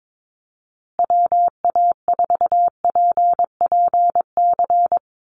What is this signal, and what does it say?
Morse code, heard as a keyed tone, about 22 wpm, 700 Hz. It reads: WA4PPC
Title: USA Call Signs Repeated 2x 22wpm